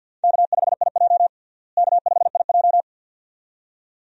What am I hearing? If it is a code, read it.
Morse code, heard as a keyed tone, about 50 wpm, 700 Hz. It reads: K5IJ K5IJ
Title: USA Call Signs - 4 Characters - Repeated 2x - 50wpm